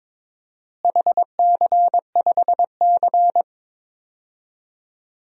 Morse code, heard as a keyed tone, about 22 wpm, 700 Hz. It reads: HC5C
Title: DXCC-100 Call Signs - 4 Characters - Repeated 3x - 22wpm